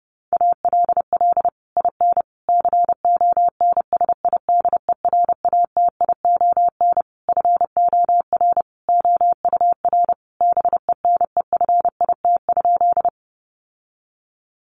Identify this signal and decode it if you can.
Morse code, heard as a keyed tone, about 30 wpm, 700 Hz. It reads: ALL IN CONSIDERATION FOR OUR BENEFIT?
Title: Sentences from Top 1500 Words 30wpm